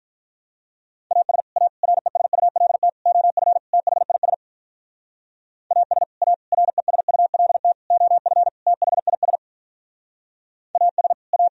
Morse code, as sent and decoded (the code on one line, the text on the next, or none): .- ... / .- / .-. ... ..- .-.. - / --- ..-. / - .... .. ... / .- ... / .- / .-. ... ..- .-.. - / --- ..-. / - .... .. ... / .- ... / .-
AS A RESULT OF THIS AS A RESULT OF THIS AS A